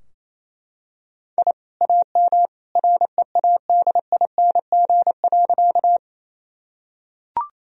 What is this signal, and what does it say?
Morse code, heard as a keyed tone, about 28 wpm, 700 Hz. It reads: I AM READING.
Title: Sentences from Top 1000 Words - Review Part 3 - 28wpm